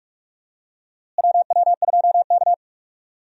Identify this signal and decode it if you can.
Morse code, heard as a keyed tone, about 41 wpm, 700 Hz. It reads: WW2K
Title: USA Call Signs - 4 Characters - Repeated 3x - 45wpm